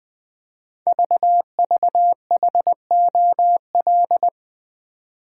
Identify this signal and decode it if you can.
Morse code, heard as a keyed tone, about 20 wpm, 700 Hz. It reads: VVHOL
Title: Code Groups - 5 Letters - 20wpm